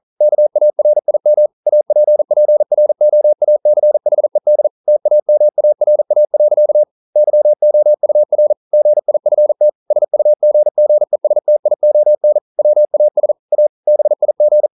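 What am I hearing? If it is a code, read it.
Morse code, heard as a keyed tone, about 44 wpm, 600 Hz. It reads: KARIM APPROACHED TAMARA. YOUR GIFT SUGGESTION WAS A BIG
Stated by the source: Morse code at 40 wpm, practice text, CW med Gnister.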